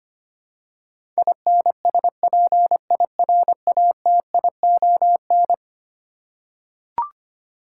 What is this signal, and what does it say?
Morse code, heard as a keyed tone, about 25 wpm, 700 Hz. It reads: INSPIRATION
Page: morse-code-ninja-practice-25wpm